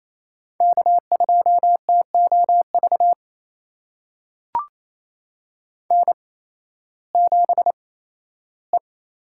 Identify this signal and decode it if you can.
Morse code, heard as a keyed tone, about 27 wpm, spaced as for 15 wpm, 700 Hz. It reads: K2TOV N7E
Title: USA Call Signs - 5 Characters - 8x Character Spacing - 28wpm